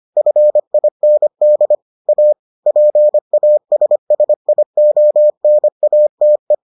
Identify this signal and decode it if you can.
Morse code, heard as a keyed tone, about 25 wpm, 600 Hz. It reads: FIND A PASSIONATE